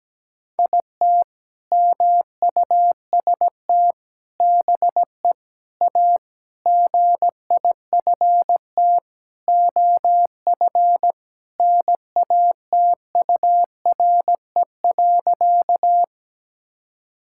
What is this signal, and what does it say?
Morse code, heard as a keyed tone, about 17 wpm, 700 Hz. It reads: IT MUST BE A GIFT OF NATURE.
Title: Sentences from Top 1000 Words - Review Part 6 - 17wpm